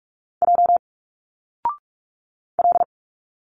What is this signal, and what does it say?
Morse code, heard as a keyed tone, about 46 wpm, 700 Hz. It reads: J L
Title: Single Letters 45wpm